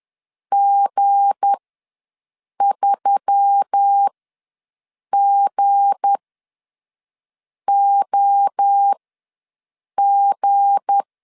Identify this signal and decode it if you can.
Morse code, heard as a keyed tone, about 10 wpm, spaced as for 5 wpm, 800 Hz. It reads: G3GOG
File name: LEZIONI / 7 GF9Z / 7 G ws 10wpm 3843